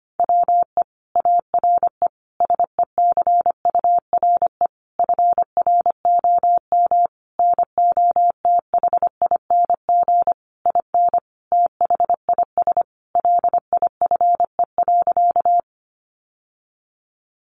Morse code, heard as a keyed tone, about 25 wpm, 700 Hz. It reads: WE ARE SECURE FROM NOTHING IN THIS LIFE.
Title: Sentences from Top 1200 Words 25wpm